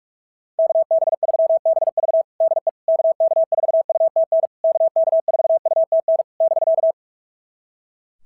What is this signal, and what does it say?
Morse code, heard as a keyed tone, about 44 wpm, 660 Hz. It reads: KB3BV DE KK4UTN KK4UTN <BK>
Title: SOTA QSO Practice 45wpm